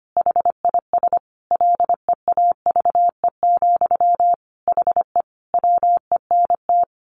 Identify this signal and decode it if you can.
Morse code, heard as a keyed tone, about 25 wpm, 700 Hz. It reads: HIS LEAVE, HE WENT
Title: Sentences from Top 1200 Words 25wpm